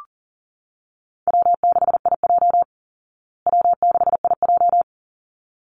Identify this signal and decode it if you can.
Morse code, heard as a keyed tone, about 40 wpm, 700 Hz. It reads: W6IJ W6IJ
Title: Easy Call Signs Repeated 2x - 40wpm